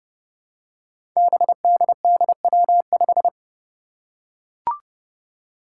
Morse code, heard as a keyed tone, about 30 wpm, 700 Hz. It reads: BDDW5
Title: Five Challenging Letter-Number Combinations 30wpm